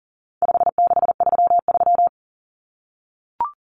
Morse code, heard as a keyed tone, about 40 wpm, 700 Hz. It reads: 5633
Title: Numbers - 4-Digit 40wpm